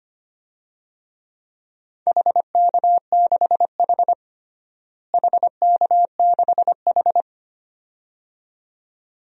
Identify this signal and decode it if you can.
Morse code, heard as a keyed tone, about 25 wpm, 700 Hz. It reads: HK6H HK6H
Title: DXCC-100 Call Signs - 4 Characters - Repeated 2x - 25wpm